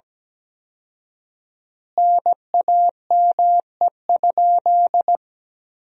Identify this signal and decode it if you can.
Morse code, heard as a keyed tone, about 17 wpm, 700 Hz. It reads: NAME?
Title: 4x Word-Spacing - Sentences from Top 400 Words - 17wpm